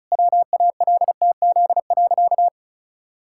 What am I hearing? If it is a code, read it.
Morse code, heard as a keyed tone, about 36 wpm, 700 Hz. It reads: WALTZ.
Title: Pangram Challenge v2 - 35wpm